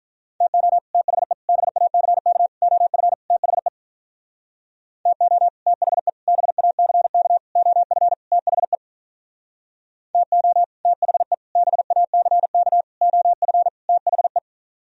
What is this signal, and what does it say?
Morse code, heard as a keyed tone, about 45 wpm, 700 Hz. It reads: TO THE BACK OF THE TO THE BACK OF THE TO THE BACK OF THE